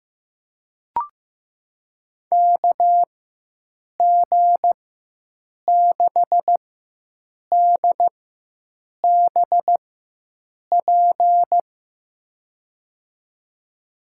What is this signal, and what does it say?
Morse code, heard as a keyed tone, about 15 wpm, spaced as for 7 wpm, 700 Hz. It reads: KG6DBP
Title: USA Call Signs - 6 Characters - 4x Character Spacing - 15wpm